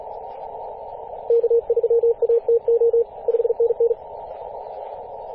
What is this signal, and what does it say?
Morse code, heard as a keyed tone, about 37 wpm, 470 Hz, with radio noise. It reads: K3ATO 5NN
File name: posts / 2016-apr-22-2153z-kh8-kc0w-18-080mhz-cw